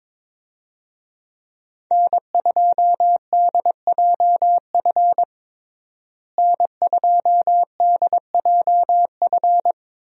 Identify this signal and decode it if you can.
Morse code, heard as a keyed tone, about 22 wpm, 700 Hz. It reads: N2DJF N2DJF
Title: USA Call Signs - 5 Characters - Repeated 3x - 22wpm